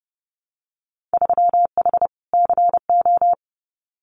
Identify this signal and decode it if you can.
Morse code, heard as a keyed tone, about 30 wpm, 700 Hz. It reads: 3H CO
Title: Winter Field Day Exchanges 30wpm